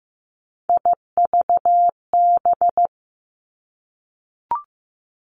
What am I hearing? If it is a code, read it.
Morse code, heard as a keyed tone, about 15 wpm, 700 Hz. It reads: IVB
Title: Three Challenging Letter-Number Combinations 15wpm